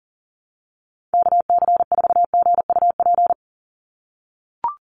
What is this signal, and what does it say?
Morse code, heard as a keyed tone, about 40 wpm, 700 Hz. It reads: KC4GUP
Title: USA Call Signs 40wpm